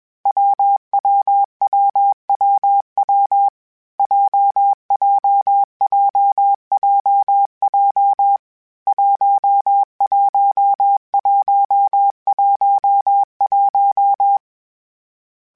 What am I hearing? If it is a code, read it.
Morse code, heard as a keyed tone, about 21 wpm, 800 Hz. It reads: WWWWW JJJJJ 11111
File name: Variable Speed / Variable Speed-mp3s / WPJ WPJ1 sp @20